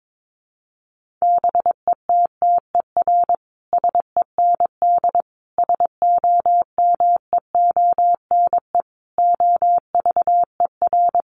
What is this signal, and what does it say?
Morse code, heard as a keyed tone, about 22 wpm, 700 Hz. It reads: BETTER SEND SOMEONE OVER
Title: Sentences from Top 1500 Words 22wpm